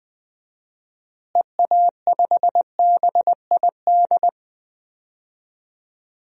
Morse code, heard as a keyed tone, about 20 wpm, 700 Hz. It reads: EA5BID
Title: DXCC-100 Call Signs - 6 Characters - Repeated 2x - 20wpm